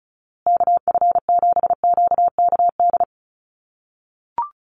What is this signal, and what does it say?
Morse code, heard as a keyed tone, about 34 wpm, 700 Hz. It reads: KF7QKD
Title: USA Call Signs Repeated 2x 35wpm